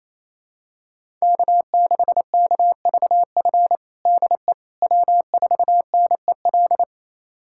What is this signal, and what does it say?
Morse code, heard as a keyed tone, about 28 wpm, 700 Hz. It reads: K6KVF DE W4NEL
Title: USA Callsign DE Callsign - 5 Characters - 28wpm